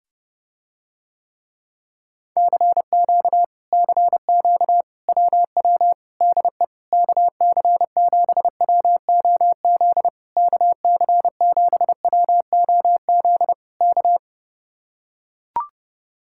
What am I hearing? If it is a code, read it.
Morse code, heard as a keyed tone, about 30 wpm, 700 Hz. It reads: CQ CQ WW DE KC7WOZ KC7WOZ K